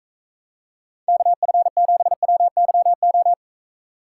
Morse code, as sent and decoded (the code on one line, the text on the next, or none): -.- ..-. --... .-- -.-- ---
KF7WYO